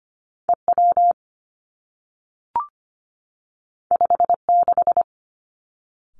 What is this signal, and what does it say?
Morse code, heard as a keyed tone, about 25 wpm, 700 Hz. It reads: E W 5 6
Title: Two Challenging Letter-Number Combinations 25wpm